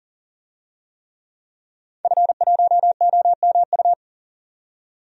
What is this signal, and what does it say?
Morse code, heard as a keyed tone, about 40 wpm, 700 Hz. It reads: F1OMU
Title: DXCC-100 Call Signs - 5 Characters - 40wpm